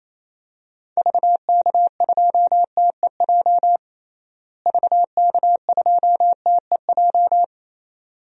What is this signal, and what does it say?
Morse code, heard as a keyed tone, about 28 wpm, 700 Hz. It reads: VK2TEJ VK2TEJ
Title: DXCC-100 Call Signs - 6 Characters - Repeated 2x - 28wpm